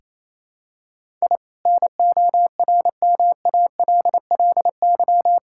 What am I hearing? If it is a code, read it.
Morse code, heard as a keyed tone, about 28 wpm, 700 Hz. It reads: I NORMALLY
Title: Sentences from Top 1000 Words - Review Part 6 - 28wpm